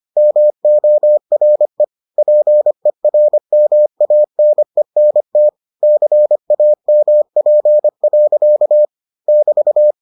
Morse code, heard as a keyed tone, about 25 wpm, 600 Hz. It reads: MORE PERMANENT CAMP. <BT>